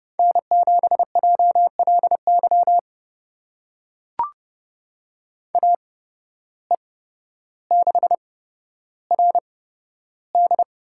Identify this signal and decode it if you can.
Morse code, heard as a keyed tone, about 30 wpm, spaced as for 13 wpm, 700 Hz. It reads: N7JLY AE6RD
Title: USA Call Signs - 8x Character Spacing - 30wpm